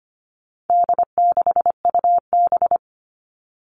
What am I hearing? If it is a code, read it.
Morse code, heard as a keyed tone, about 25 wpm, 700 Hz. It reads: D6UB
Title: All Cut Numbers - 4-Digit 25wpm